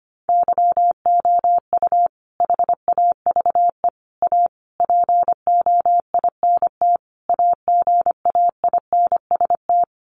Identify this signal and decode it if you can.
Morse code, heard as a keyed tone, about 25 wpm, 700 Hz. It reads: YOU HAVE A POINT AGAINST
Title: Sentences from Top 1400 Words 25wpm